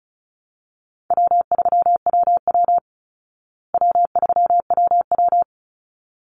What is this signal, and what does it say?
Morse code, heard as a keyed tone, about 34 wpm, 700 Hz. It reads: W3WW W3WW
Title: Call Signs Repeated 3x - 35wpm